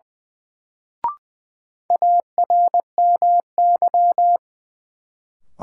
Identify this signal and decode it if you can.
Morse code, heard as a keyed tone, about 21 wpm, 700 Hz. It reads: ARMY
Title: Sentences from Top 800 Words - New Word Drill - 20wpm